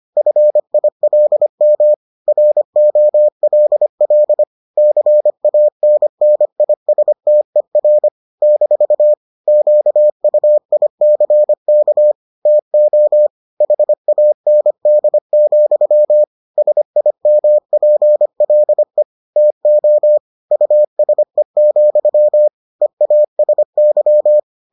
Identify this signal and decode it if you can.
Morse code, heard as a keyed tone, about 25 wpm, 600 Hz. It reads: FILM ROLL CANNISTER - QUICK TO HAND, SIMPLE TO USE, EASY